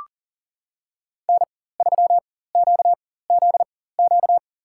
Morse code, heard as a keyed tone, about 40 wpm, spaced as for 19 wpm, 700 Hz. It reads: N3QZQ
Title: USA Call Signs - 4x Character Spacing - 40wpm